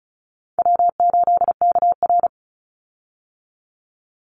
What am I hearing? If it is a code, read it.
Morse code, heard as a keyed tone, about 34 wpm, 700 Hz. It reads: W8KR
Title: Call Signs - 35wpm